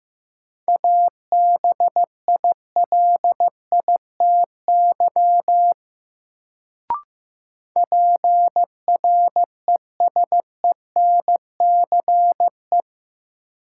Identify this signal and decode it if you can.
Morse code, heard as a keyed tone, about 15 wpm, 700 Hz. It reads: ABILITY PRESENCE
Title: Sets of 3 Words - New Word Drill - 15wpm